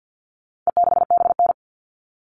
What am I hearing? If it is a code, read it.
Morse code, heard as a keyed tone, about 50 wpm, 700 Hz. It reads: E6BN
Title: All Cut Numbers - 4-Digit 50wpm